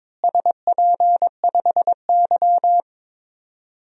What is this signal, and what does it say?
Morse code, heard as a keyed tone, about 22 wpm, 700 Hz. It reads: SP5Y
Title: DXCC-100 Call Signs - 4 Characters - Repeated 2x - 22wpm